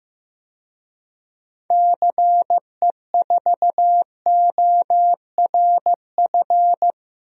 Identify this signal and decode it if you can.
Morse code, heard as a keyed tone, about 15 wpm, 700 Hz. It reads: CE4ORF
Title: DXCC-100 Call Signs - 6 Characters - Repeated 3x - 15wpm